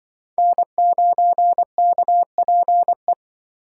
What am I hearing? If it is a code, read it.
Morse code, heard as a keyed tone, about 23 wpm, 700 Hz. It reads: N9KPE
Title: Speed Racing - USA 1x3 Call Signs 20wpm